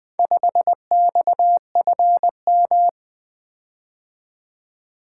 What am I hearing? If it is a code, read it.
Morse code, heard as a keyed tone, about 24 wpm, 700 Hz. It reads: 5XFM
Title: Speed-Racing - Four Letter-Number Combinations - 20wpm